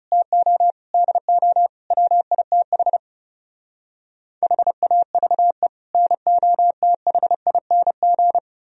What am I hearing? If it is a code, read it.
Morse code, heard as a keyed tone, about 38 wpm, 700 Hz. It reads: TO DO WITH HAVE NOTHING